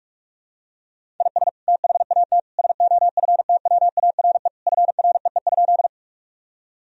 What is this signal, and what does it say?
Morse code, heard as a keyed tone, about 47 wpm, 700 Hz. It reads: IS THAT SOFTWARE FREE?
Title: Sentences from Top 1000 Words - Review Part 3 - 45wpm